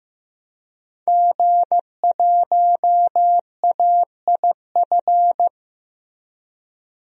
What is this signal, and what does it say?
Morse code, heard as a keyed tone, about 15 wpm, 700 Hz. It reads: G1AIF
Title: DXCC-100 Call Signs - 5 Characters - Repeated 3x - 15wpm